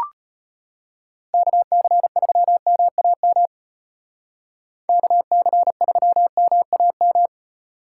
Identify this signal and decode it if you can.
Morse code, heard as a keyed tone, about 30 wpm, 700 Hz. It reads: KC3MAM KC3MAM
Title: Speed Racing - USA 2x3 Call Signs 25wpm